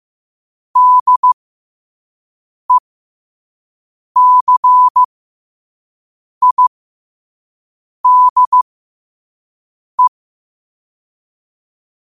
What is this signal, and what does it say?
Morse code, heard as a keyed tone, about 15 wpm, spaced as for 5 wpm, 1000 Hz. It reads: DECIDE
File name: qotdfiles / 5wpm